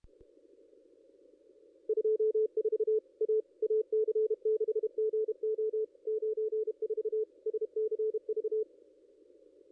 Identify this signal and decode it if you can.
Morse code, heard as a keyed tone, about 33 wpm, 430 Hz. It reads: 24 A AC6GO 94 SCV
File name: contests / ss-cw-k5tr-op-n5rz -2200